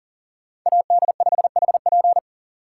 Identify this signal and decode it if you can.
Morse code, heard as a keyed tone, about 50 wpm, 700 Hz. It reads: AD5HP